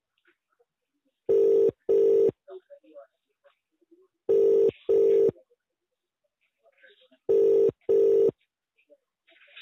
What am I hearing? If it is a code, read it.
Ringback tone (UK), 400 + 450 Hz, with speech.